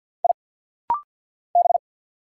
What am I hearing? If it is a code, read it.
Morse code, heard as a keyed tone, about 50 wpm, 700 Hz. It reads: I B
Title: Single Letters - Rapid-Fire 50wpm